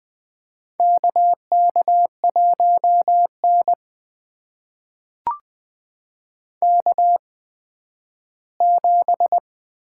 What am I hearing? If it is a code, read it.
Morse code, heard as a keyed tone, about 20 wpm, spaced as for 12 wpm, 700 Hz. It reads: KK1N K7
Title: USA Call Signs - 4 Characters - 8x Character Spacing - 20wpm